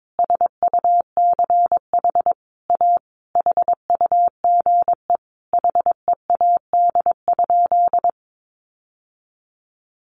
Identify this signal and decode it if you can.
Morse code, heard as a keyed tone, about 22 wpm, 700 Hz. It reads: SUCH A HUGE HEAD?
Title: Sentences from Top 1100 Words 22wpm